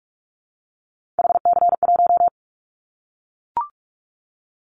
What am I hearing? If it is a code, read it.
Morse code, heard as a keyed tone, about 44 wpm, 700 Hz. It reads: HC1
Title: Callsign Prefixes for Top 100 DXCC - 45wpm